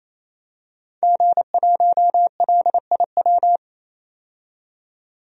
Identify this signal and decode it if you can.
Morse code, heard as a keyed tone, about 28 wpm, 700 Hz. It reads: G1LIW
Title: DXCC-100 Call Signs - 5 Characters - 28wpm